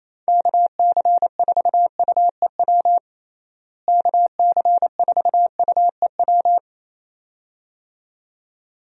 Morse code, heard as a keyed tone, about 28 wpm, 700 Hz. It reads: KC4UEW KC4UEW
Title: USA Call Signs - 6 Characters - Repeated 2x - 28wpm